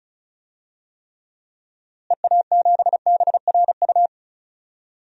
Morse code, heard as a keyed tone, about 36 wpm, 700 Hz. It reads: EA7BRU
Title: DXCC-100 Call Signs - 6 Characters - Repeated 3x - 35wpm